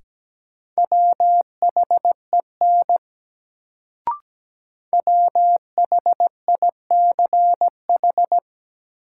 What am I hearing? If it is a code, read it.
Morse code, heard as a keyed tone, about 17 wpm, 700 Hz. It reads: WHEN WHICH
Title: Sentences from Top 100 Words - New Word Drill - 17wpm